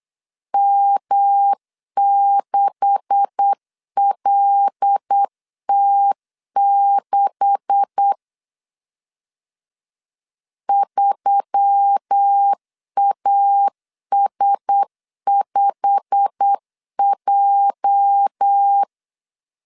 Morse code, heard as a keyed tone, about 8 wpm, 800 Hz. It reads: M6LT6 3AS5J
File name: CW / DET / 1 9 1500C S B 8wpm 5624